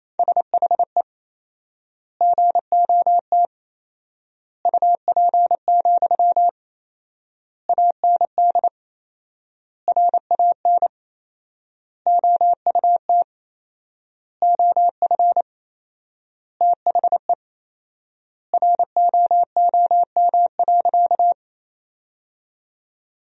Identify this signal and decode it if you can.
Morse code, heard as a keyed tone, about 28 wpm, 700 Hz. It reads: SHE GOT UP, AND RAN OUT OF THE ROOM.
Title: 4x Word-Spacing - Sentences from Top 500 Words - 28wpm